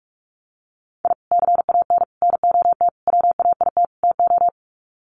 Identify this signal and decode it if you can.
Morse code, heard as a keyed tone, about 46 wpm, 700 Hz. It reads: I CAN NOT WAIT TO
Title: Sets of 5 Words Encore 45wpm